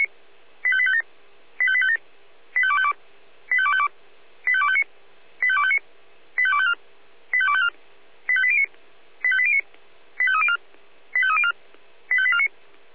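Five-tone selective call (ZVEI), heard as quick runs of five tones, tones of about 70 ms. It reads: 85857 85857 85252 85252 85259 85259 85254 85254 85809 85809 85304 85304 85849
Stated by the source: ZVEI five-tone call received by SDR, recorded by feuerwehr-orlamuende (fire brigade Orlamünde).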